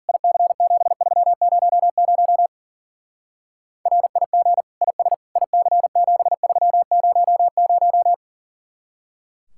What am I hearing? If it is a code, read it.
Morse code, heard as a keyed tone, about 50 wpm, 700 Hz. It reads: IC7300 RIG IS IC7300